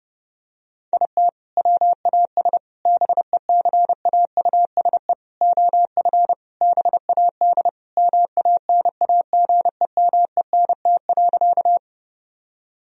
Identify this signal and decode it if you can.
Morse code, heard as a keyed tone, about 30 wpm, 700 Hz. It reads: IT WAS BECAUSE OF BAD MANAGEMENT.
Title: Sentences from Top 1000 Words - Review Part 4 - 30wpm